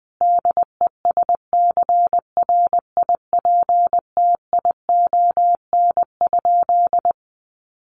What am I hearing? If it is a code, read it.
Morse code, heard as a keyed tone, about 20 wpm, 700 Hz. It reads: DESCRIPTION?
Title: Sentences from Top 1300 Words 20wpm